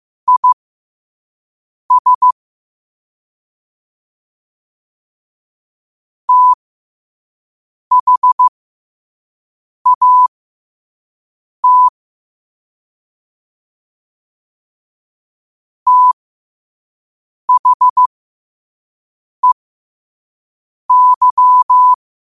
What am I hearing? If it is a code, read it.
Morse code, heard as a keyed tone, about 15 wpm, spaced as for 5 wpm, 1000 Hz. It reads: IS THAT THEY
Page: Quotes for Wed, 13 Aug 2025 in Morse Code at 5 words per minute.